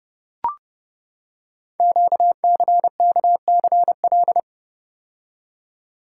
Morse code, heard as a keyed tone, about 30 wpm, 700 Hz. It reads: QCKCL
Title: Code Groups - 5 Letters - 30wpm